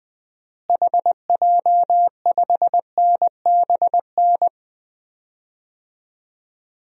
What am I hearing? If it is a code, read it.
Morse code, heard as a keyed tone, about 21 wpm, 700 Hz. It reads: HJ5NBN
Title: DXCC-100 Call Signs - 6 Characters - Repeated 2x - 20wpm